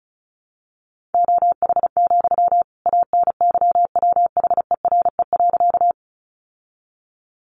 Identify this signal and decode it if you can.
Morse code, heard as a keyed tone, about 34 wpm, 700 Hz. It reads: OH, ANYWHERE.
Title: Sentences from Top 1400 Words 35wpm